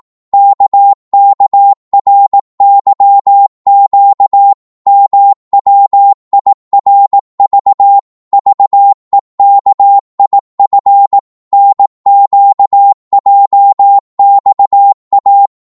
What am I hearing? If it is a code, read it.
Morse code, heard as a keyed tone, about 18 wpm, 800 Hz. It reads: KKRYQ MWIRV VEKIF NQJXA
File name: Morse / ltr-18wpm